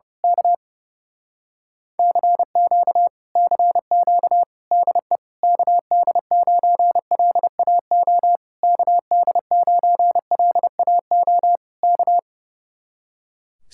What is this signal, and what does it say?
Morse code, heard as a keyed tone, about 28 wpm, 700 Hz. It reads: K CQ CQ DE KD9LAO KD9LAO K
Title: Speed-Racing Common QSO Phrases - 30wpm